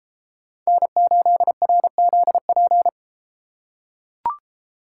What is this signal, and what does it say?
Morse code, heard as a keyed tone, about 31 wpm, 700 Hz. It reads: N8RZP